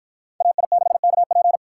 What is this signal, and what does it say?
Morse code, heard as a keyed tone, about 52 wpm, 700 Hz. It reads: AI6KP